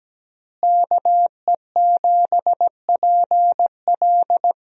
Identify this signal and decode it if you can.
Morse code, heard as a keyed tone, about 17 wpm, 700 Hz. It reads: KE7PL
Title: USA Call Signs - 5 Characters - Repeated 2x - 17wpm